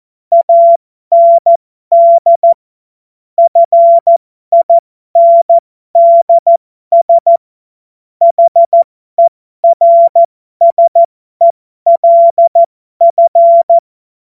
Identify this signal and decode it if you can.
Morse code, heard as a keyed tone, about 14 wpm, spaced as for 12 wpm, 680 Hz. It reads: AND FINDS HERSELF
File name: Morsekurs / Witze